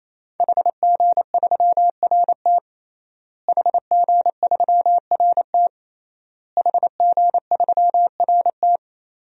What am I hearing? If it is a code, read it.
Morse code, heard as a keyed tone, about 28 wpm, 700 Hz. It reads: HG3RT HG3RT HG3RT
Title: DXCC-100 Call Signs - 5 Characters - Repeated 3x - 28wpm